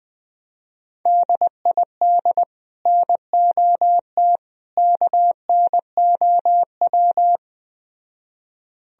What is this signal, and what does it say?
Morse code, heard as a keyed tone, about 20 wpm, 700 Hz. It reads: DID NOT KNOW